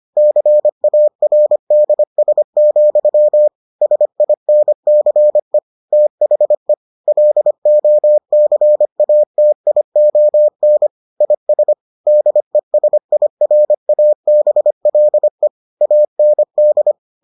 Morse code, heard as a keyed tone, about 25 wpm, 600 Hz. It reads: CARDS, SINCE THE LOCATION IS DESIRABLE AND